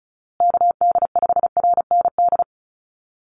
Morse code, heard as a keyed tone, about 34 wpm, 700 Hz. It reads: KD5RND
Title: USA Call Signs Repeated 2x 35wpm